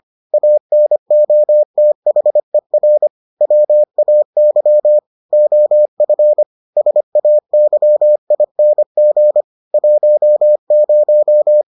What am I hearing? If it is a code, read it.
Morse code, heard as a keyed tone, about 25 wpm, 600 Hz. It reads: ANOTHER WAY OF SAYING 10